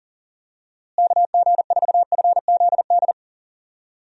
Morse code, heard as a keyed tone, about 39 wpm, 700 Hz. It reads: KG4FZD
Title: USA Call Signs - 6 Characters - Repeated 2x - 40wpm